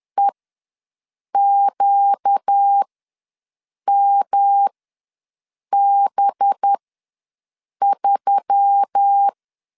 Morse code, heard as a keyed tone, about 10 wpm, spaced as for 6 wpm, 800 Hz. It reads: EQMB3
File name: LEZIONI / 5 LBVC / 5 LB ws 10wpm 3616